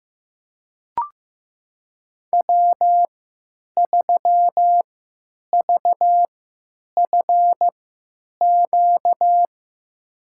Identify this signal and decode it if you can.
Morse code, heard as a keyed tone, about 15 wpm, spaced as for 9 wpm, 700 Hz. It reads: W3VFQ
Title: USA Call Signs - 5 Characters - 3x Character Spacing - 15wpm